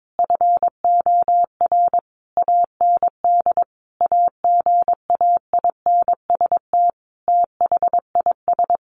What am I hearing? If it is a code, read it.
Morse code, heard as a keyed tone, about 22 wpm, 700 Hz. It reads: FOR AND AGAINST THIS
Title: Sentences from Top 1300 Words 22wpm